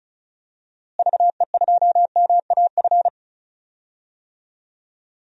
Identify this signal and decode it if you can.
Morse code, heard as a keyed tone, about 36 wpm, 700 Hz. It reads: VE2MAF
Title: DXCC-100 Call Signs - 6 Characters - Repeated 3x - 35wpm